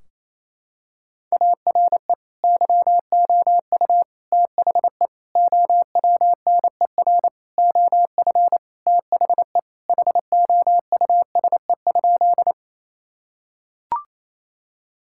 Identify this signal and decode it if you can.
Morse code, heard as a keyed tone, about 28 wpm, 700 Hz. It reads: ARE YOU THE OWNER OF THE HOUSE?
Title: Sentences from Top 1600 Words 28wpm